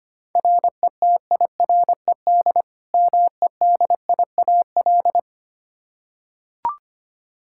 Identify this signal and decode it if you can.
Morse code, heard as a keyed tone, about 25 wpm, 700 Hz. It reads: RETIRED MEDIAL
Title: Practice QSO - Exchange 3 - More Info - 25wpm